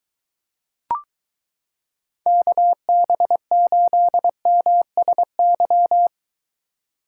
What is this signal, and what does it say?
Morse code, heard as a keyed tone, about 22 wpm, 700 Hz. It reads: KB8MSY